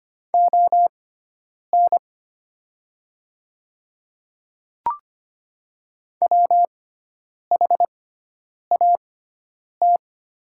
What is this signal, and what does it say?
Morse code, heard as a keyed tone, about 25 wpm, spaced as for 9 wpm, 700 Hz. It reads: ON WHAT
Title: 6x Character Spacing - Top 300 Words - 25wpm